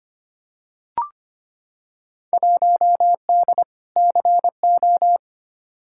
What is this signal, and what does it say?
Morse code, heard as a keyed tone, about 25 wpm, 700 Hz. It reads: 1D CO
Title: Field Day Exchanges - 25wpm